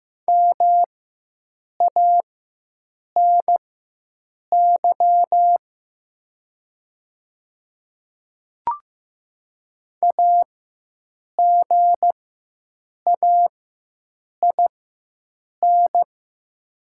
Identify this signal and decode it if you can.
Morse code, heard as a keyed tone, about 15 wpm, spaced as for 7 wpm, 700 Hz. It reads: MANY AGAIN
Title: 4x Character Spacing - Top 200 Words - 15wpm